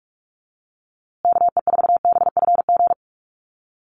Morse code, heard as a keyed tone, about 46 wpm, 690 Hz. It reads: KE4BFG
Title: USA Call Signs Repeated 3x 45wpm